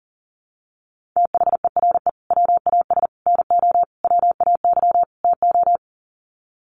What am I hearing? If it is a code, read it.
Morse code, heard as a keyed tone, about 37 wpm, 700 Hz. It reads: THERE WAS NO WAY TO
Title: Sets of 5 Words Encore 40wpm